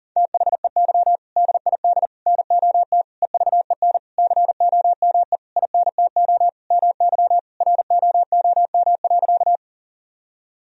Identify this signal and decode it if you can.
Morse code, heard as a keyed tone, about 40 wpm, 700 Hz. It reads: THEY DID NOT EVEN COME INTO MY ROOM.
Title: Sentences from Top 1000 Words - Review Part 1 - 40wpm